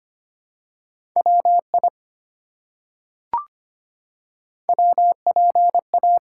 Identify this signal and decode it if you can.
Morse code, heard as a keyed tone, about 25 wpm, 700 Hz. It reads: WI WPA
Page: morse-code-ninja-practice-25wpm